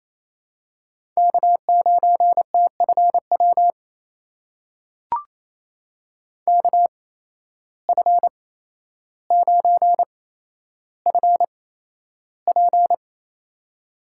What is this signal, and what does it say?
Morse code, heard as a keyed tone, about 28 wpm, spaced as for 12 wpm, 700 Hz. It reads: K9TFW KF9FP
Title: USA Call Signs - 5 Characters - 8x Character Spacing - 28wpm